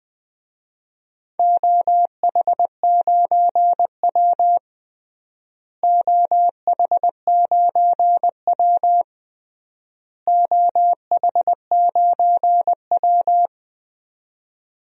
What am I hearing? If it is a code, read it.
Morse code, heard as a keyed tone, about 20 wpm, 700 Hz. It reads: OH9W OH9W OH9W
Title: DXCC-100 Call Signs - 4 Characters - Repeated 3x - 20wpm